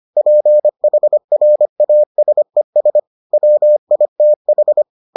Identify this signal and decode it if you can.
Morse code, heard as a keyed tone, about 25 wpm, 600 Hz. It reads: PHRASES WITH